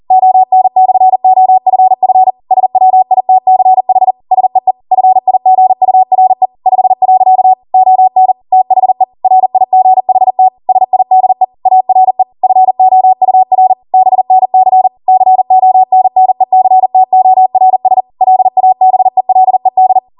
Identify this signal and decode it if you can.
Morse code, heard as a keyed tone, about 40 wpm, 750 Hz. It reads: ON/OFF SWITCH SEE FIGURE 5. ON THE RIGHT SIDE ARE FOUR BNC CONNECTORS LABELED